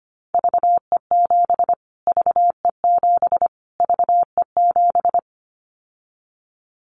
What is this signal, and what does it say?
Morse code, heard as a keyed tone, about 25 wpm, 700 Hz. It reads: VE7 VE7 VE7
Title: Callsign Prefixes for Top 25 DXCC - Repeated 3x - 25wpm